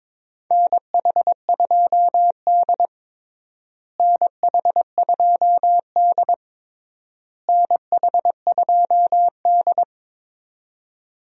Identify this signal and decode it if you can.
Morse code, heard as a keyed tone, about 21 wpm, 700 Hz. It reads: NH2D NH2D NH2D
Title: USA Call Signs - 4 Characters - Repeated 3x - 22wpm